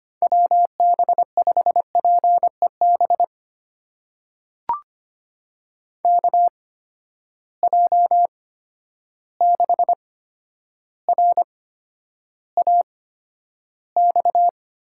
Morse code, heard as a keyed tone, about 25 wpm, spaced as for 11 wpm, 700 Hz. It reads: WB5PEB KJ6RAX
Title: USA Call Signs - 8x Character Spacing - 25wpm